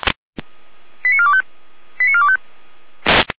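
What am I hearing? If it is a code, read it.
Five-tone selective call (ZVEI), heard as quick runs of five tones, tones of about 70 ms. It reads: 89425 89425
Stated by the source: ZVEI five-tone selective call received by SDR, fire brigade alerting.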